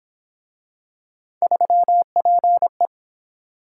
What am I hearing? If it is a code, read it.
Morse code, heard as a keyed tone, about 27 wpm, 700 Hz. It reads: 3PE
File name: Speed-Racing - Three Letter-Number Combinations - 22wpm